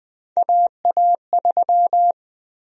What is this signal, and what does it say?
Morse code, heard as a keyed tone, about 20 wpm, 700 Hz. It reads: AA3
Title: USA Call Sign Prefixes - Repeated 2x - 20wpm